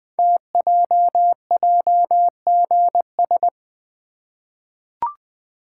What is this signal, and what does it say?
Morse code, heard as a keyed tone, about 20 wpm, 700 Hz. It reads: TJJGS
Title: Code Groups - 5 Letters - 20wpm